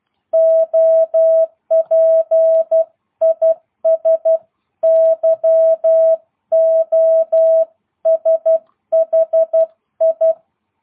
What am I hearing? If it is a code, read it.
Morse code, heard as a keyed tone, about 12 wpm, 650 Hz. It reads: OP IS YOSHI
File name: jh8cbh / 2022cw